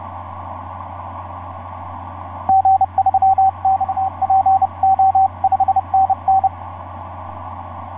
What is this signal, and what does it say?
Morse code, heard as a keyed tone, about 30 wpm, 770 Hz, with radio noise. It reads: G3XPO 5NN
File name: s79gm f6eyb etc 12m cw 24mar2010 1207